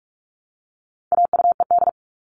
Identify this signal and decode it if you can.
Morse code, heard as a keyed tone, about 44 wpm, 700 Hz. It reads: AUED
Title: All Cut Numbers - 4-Digit 45wpm